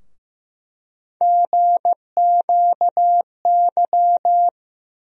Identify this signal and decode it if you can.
Morse code, heard as a keyed tone, about 15 wpm, 700 Hz. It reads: GQY
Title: USA Call Sign Postfixes - 3 Letters 15wpm